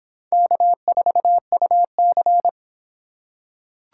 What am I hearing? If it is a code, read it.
Morse code, heard as a keyed tone, about 25 wpm, 700 Hz. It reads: K4UC